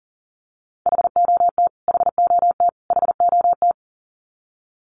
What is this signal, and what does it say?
Morse code, heard as a keyed tone, about 40 wpm, 700 Hz. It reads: HOT HOT HOT
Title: Top 500 Words Repeated 40wpm